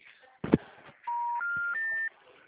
Special information tone.
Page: Marilia - SP